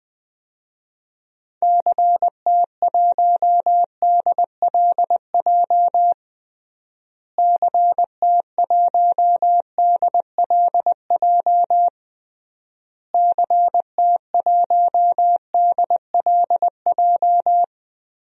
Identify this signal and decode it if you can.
Morse code, heard as a keyed tone, about 20 wpm, 700 Hz. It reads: CT1DLJ CT1DLJ CT1DLJ
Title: DXCC-100 Call Signs - 6 Characters - Repeated 3x - 20wpm